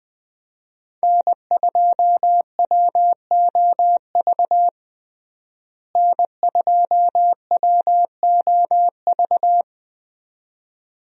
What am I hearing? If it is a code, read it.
Morse code, heard as a keyed tone, about 21 wpm, 700 Hz. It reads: N2WOV N2WOV
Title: USA Call Signs - 5 Characters - Repeated 2x - 20wpm